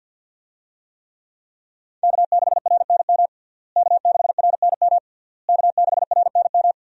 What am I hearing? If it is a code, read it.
Morse code, heard as a keyed tone, about 53 wpm, 700 Hz. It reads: K6RNM K6RNM K6RNM
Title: USA Call Signs - 5 Characters - Repeated 3x - 50wpm